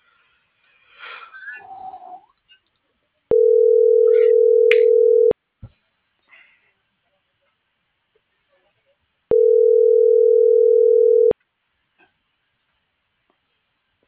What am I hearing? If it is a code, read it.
Ringback tone (North America), 440 + 480 Hz.